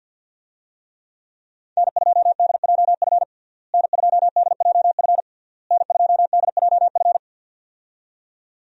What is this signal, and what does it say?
Morse code, heard as a keyed tone, about 52 wpm, 700 Hz. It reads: N2DJF N2DJF N2DJF
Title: USA Call Signs - 5 Characters - Repeated 3x - 50wpm